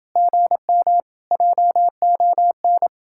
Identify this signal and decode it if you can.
Morse code, heard as a keyed tone, about 28 wpm, 700 Hz. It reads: GM JON